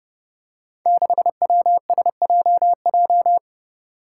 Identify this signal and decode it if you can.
Morse code, heard as a keyed tone, about 31 wpm, 700 Hz. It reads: 6WSJJ
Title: Five Challenging Letter-Number Combinations 30wpm